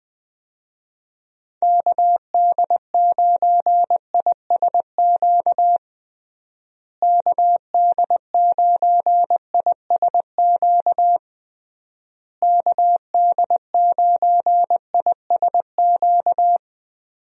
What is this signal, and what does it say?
Morse code, heard as a keyed tone, about 21 wpm, 700 Hz. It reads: KD9ISQ KD9ISQ KD9ISQ
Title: USA Call Signs - 6 Characters - Repeated 3x - 20wpm